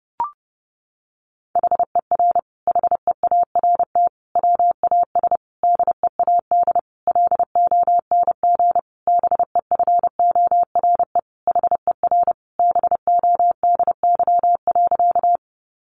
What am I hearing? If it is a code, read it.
Morse code, heard as a keyed tone, about 30 wpm, 700 Hz. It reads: HER HEART WAS DEAD LONG BEFORE HER BODY.
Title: Sentences from Top 1500 Words 30wpm